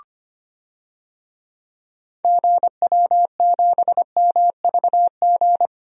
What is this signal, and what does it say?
Morse code, heard as a keyed tone, about 25 wpm, 700 Hz. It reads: GW7MVG
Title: DXCC-100 Call Signs - 6 Characters - 25wpm